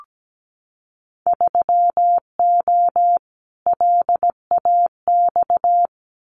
Morse code, heard as a keyed tone, about 17 wpm, 700 Hz. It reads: 3O LAX
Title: Winter Field Day Exchanges Encore 17wpm